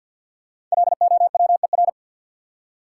Morse code, heard as a keyed tone, about 53 wpm, 700 Hz. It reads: LOWER
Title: Five Letter Words - 50wpm